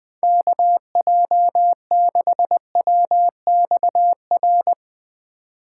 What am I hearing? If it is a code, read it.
Morse code, heard as a keyed tone, about 20 wpm, 700 Hz. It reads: KJ6WXR
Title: Speed Racing - USA 2x3 Call Signs 17wpm